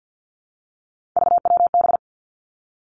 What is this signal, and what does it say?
Morse code, heard as a keyed tone, about 50 wpm, 700 Hz. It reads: VWB
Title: Three Challenging Letter-Number Combinations 50wpm